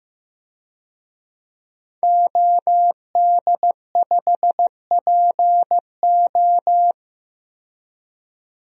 Morse code, heard as a keyed tone, about 15 wpm, 700 Hz. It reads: OD5PO
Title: DXCC-100 Call Signs - 5 Characters - 15wpm